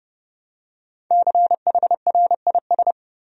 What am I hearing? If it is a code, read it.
Morse code, heard as a keyed tone, about 30 wpm, 700 Hz. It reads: CHRIS
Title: USA Common Names - Max 5 Letters - 30wpm